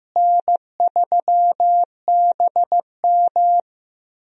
Morse code, heard as a keyed tone, about 15 wpm, 700 Hz. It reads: N3BM
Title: USA Call Signs - 4 Characters - Repeated 2x - 15wpm